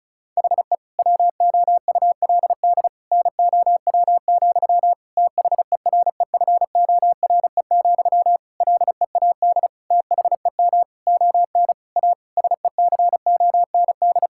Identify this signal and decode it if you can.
Morse code, heard as a keyed tone, about 34 wpm, 700 Hz. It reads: HE WOULD NOW, THEREFORE, LEAD THEM ON A SECOND
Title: Sentences from Top 1000 Words - Review Part 5 - 35wpm